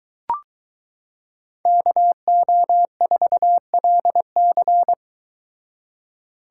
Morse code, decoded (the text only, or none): KO4LC